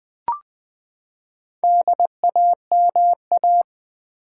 Morse code, heard as a keyed tone, about 20 wpm, 700 Hz. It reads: DAMA
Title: Build A Word - 4 Letters At A Time - 8-Letter Words - 20wpm